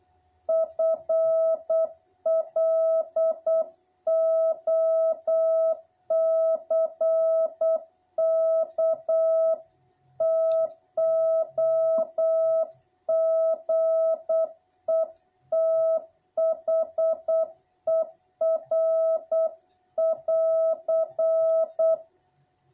Morse code, heard as a keyed tone, about 8 wpm, 660 Hz. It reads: FLOCK TOGETHER<AR>
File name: jh8cbh / 2021cw